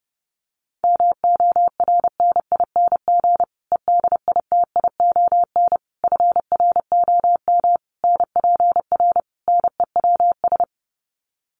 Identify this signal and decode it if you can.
Morse code, heard as a keyed tone, about 30 wpm, 700 Hz. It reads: MORNING EDITION FROM NPR NEWS
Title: Sets of 5 Words Encore 30wpm